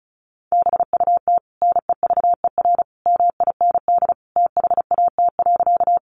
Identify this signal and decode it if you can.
Morse code, heard as a keyed tone, about 34 wpm, 700 Hz. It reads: BUT NEVER MIND THAT.
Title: Sentences from Top 500 Words 35wpm